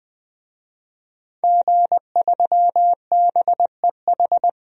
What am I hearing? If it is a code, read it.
Morse code, heard as a keyed tone, about 20 wpm, 700 Hz. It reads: G3BEH
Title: DXCC-100 Call Signs - 5 Characters - 20wpm